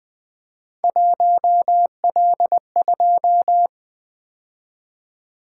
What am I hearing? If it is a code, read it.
Morse code, heard as a keyed tone, about 20 wpm, 700 Hz. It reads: 1L2
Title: Three Letter-Number Combinations - 20wpm